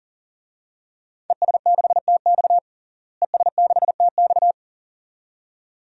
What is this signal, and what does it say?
Morse code, heard as a keyed tone, about 40 wpm, 700 Hz. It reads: ES6TX ES6TX
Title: DXCC-100 Call Signs - 5 Characters - Repeated 2x - 40wpm